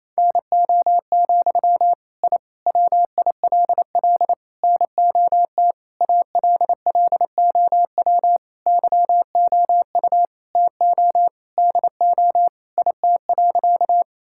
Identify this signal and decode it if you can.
Morse code, heard as a keyed tone, about 28 wpm, 700 Hz. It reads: NO, I WILL NOT ALLOW YOU TO DO IT.
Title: Sentences from Top 1000 Words - Review Part 5 - 28wpm